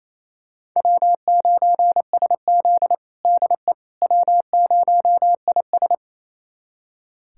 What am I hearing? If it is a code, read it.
Morse code, heard as a keyed tone, about 28 wpm, 700 Hz. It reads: W9SZ DE W0IS
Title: USA Callsign DE Callsign - 4 Characters - 28wpm